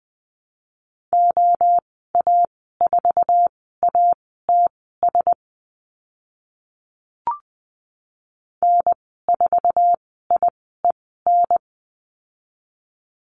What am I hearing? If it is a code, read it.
Morse code, heard as a keyed tone, about 20 wpm, spaced as for 14 wpm, 700 Hz. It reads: OA4ATS N4IEN
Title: 062 Lesson 11g - Letter h - Call Signs 20wpm